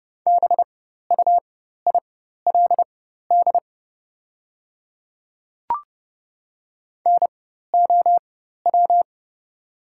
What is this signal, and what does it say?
Morse code, heard as a keyed tone, about 30 wpm, spaced as for 14 wpm, 700 Hz. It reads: BUILD NOW
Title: 4x Character Spacing - Top 300 Words - 30wpm